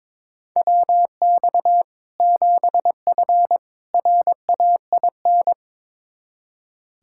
Morse code, heard as a keyed tone, about 23 wpm, 700 Hz. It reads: WX 7F RAIN
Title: Practice QSO - Exchange 2 - Info - 22wpm